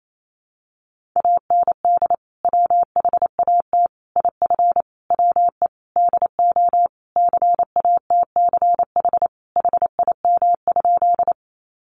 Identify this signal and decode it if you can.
Morse code, heard as a keyed tone, about 28 wpm, 700 Hz. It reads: AND WHAT IF WE DO CATCH HIM?
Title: Sentences from Top 1500 Words 28wpm